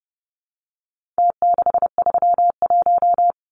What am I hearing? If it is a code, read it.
Morse code, heard as a keyed tone, about 30 wpm, 700 Hz. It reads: T631
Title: Numbers - Contest Style - 4-Digit 30wpm